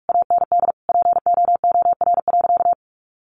Morse code, heard as a keyed tone, about 44 wpm, 700 Hz. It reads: AND POOR.
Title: Sentences from Top 1100 Words 45wpm